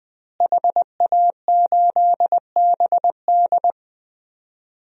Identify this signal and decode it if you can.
Morse code, heard as a keyed tone, about 21 wpm, 700 Hz. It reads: HA8BD
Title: DXCC-100 Call Signs - 5 Characters - Repeated 2x - 20wpm